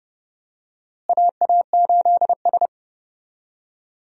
Morse code, heard as a keyed tone, about 30 wpm, 700 Hz. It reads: AA8S
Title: USA Call Signs - 4 Characters - Repeated 3x - 30wpm